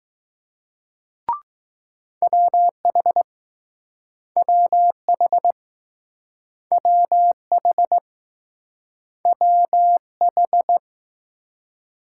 Morse code, heard as a keyed tone, about 20 wpm, 700 Hz. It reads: WH WH WH WH